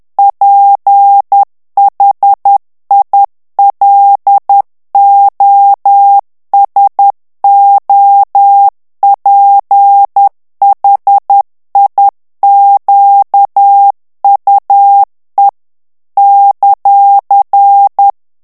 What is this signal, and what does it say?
Morse code, heard as a keyed tone, about 10 wpm, 800 Hz. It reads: PHILOSOPHIQUE ;
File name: morse mp3 / FRANCAIS TEXTE CLAIR 2 VITESSE 10 MOTS MINUTES